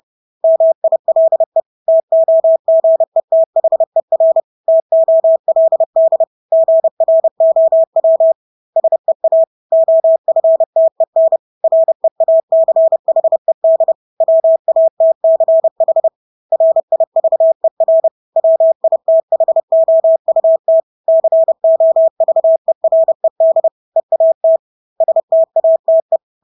Morse code, heard as a keyed tone, about 30 wpm, 650 Hz. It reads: MILE TOGETHER TOLD GROW SEA OFTEN REACHED WATCH RIVER WITHOUT COVERED EAT STATE
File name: Engelske ord 30-30 wpm